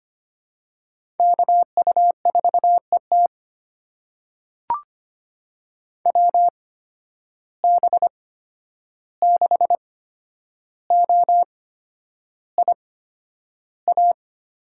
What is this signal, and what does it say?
Morse code, heard as a keyed tone, about 25 wpm, spaced as for 10 wpm, 700 Hz. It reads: KU4ET WB6OIA
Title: USA Call Signs - 8x Character Spacing - 25wpm